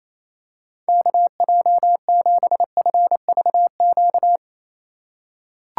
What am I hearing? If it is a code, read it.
Morse code, heard as a keyed tone, about 28 wpm, 700 Hz. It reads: KJ7FVQ
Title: USA Call Signs - 6 Characters - 4x Character Spacing - 28wpm